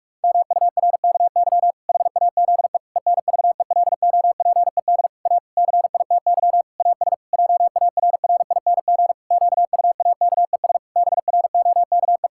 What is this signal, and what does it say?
Morse code, heard as a keyed tone, about 46 wpm, 700 Hz. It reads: MURKY HAZE ENVELOPED A CITY AS JARRING QUAKES BROKE
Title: Pangram Challenge v2 - 45wpm